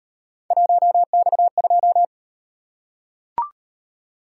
Morse code, heard as a keyed tone, about 36 wpm, 700 Hz. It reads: 1X2